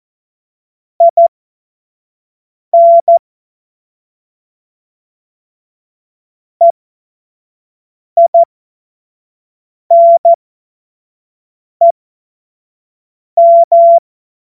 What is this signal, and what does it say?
Morse code, heard as a keyed tone, about 14 wpm, spaced as for 5 wpm, 680 Hz. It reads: IN EINEM
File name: Morsekurs / Witze / Witz 17. August 2021, 25 F70